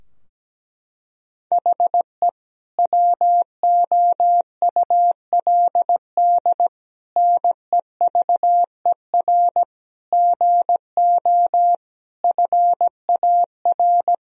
Morse code, heard as a keyed tone, about 17 wpm, 700 Hz. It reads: HE WOULD NEVER GO FAR
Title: Sentences from Top 1000 Words - Review Part 2 - 17wpm